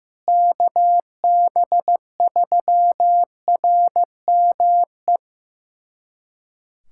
Morse code, heard as a keyed tone, about 15 wpm, 700 Hz. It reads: KB3RME
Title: USA Callsign DE Callsign - 6 Characters - 15wpm